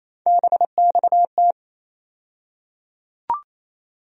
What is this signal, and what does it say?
Morse code, heard as a keyed tone, about 28 wpm, 700 Hz. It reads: BXT
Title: USA Call Sign Postfixes - 3 Letters 28wpm